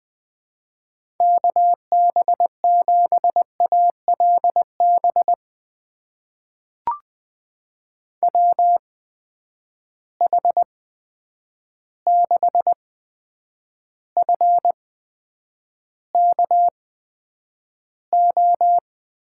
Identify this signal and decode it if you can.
Morse code, heard as a keyed tone, about 21 wpm, spaced as for 9 wpm, 700 Hz. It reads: KB7ALB WH6FKO
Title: USA Call Signs - 6 Characters - 8x Character Spacing - 20wpm